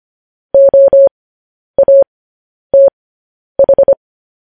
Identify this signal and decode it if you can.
Morse code, heard as a keyed tone, about 25 wpm, spaced as for 10 wpm, 550 Hz. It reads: OATH
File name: podcast / News headlines-1x-25x10